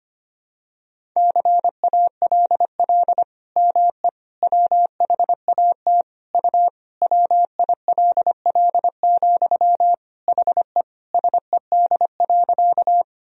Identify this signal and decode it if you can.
Morse code, heard as a keyed tone, about 25 wpm, 700 Hz. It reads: CALL ME WHAT U WILL, HE SED.
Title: Sentences from Top 200 Words and Abbreviations 25wpm